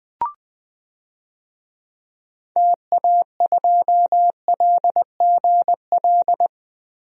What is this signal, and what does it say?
Morse code, heard as a keyed tone, about 20 wpm, 700 Hz. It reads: TA2LGL
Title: DXCC-100 Call Signs - 6 Characters - 20wpm